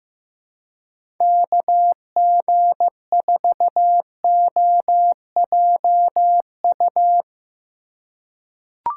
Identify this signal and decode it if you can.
Morse code, heard as a keyed tone, about 15 wpm, 700 Hz. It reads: KG4OJU
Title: USA Call Signs - 4x Character Spacing - 15wpm